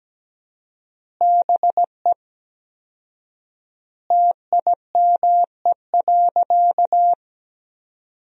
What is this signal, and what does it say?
Morse code, heard as a keyed tone, about 17 wpm, 700 Hz. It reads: BE TIME.
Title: 4x Word-Spacing - Sentences from Top 200 Words - 17wpm